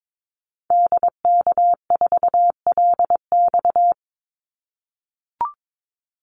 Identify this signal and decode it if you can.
Morse code, heard as a keyed tone, about 22 wpm, 700 Hz. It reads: DK4LX
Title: Call Signs - 22wpm